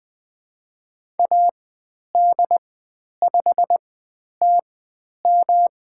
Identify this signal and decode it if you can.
Morse code, heard as a keyed tone, about 20 wpm, spaced as for 10 wpm, 700 Hz. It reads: AD5TM
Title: 109 Lesson 18g - Letter m - Call Signs 20wpm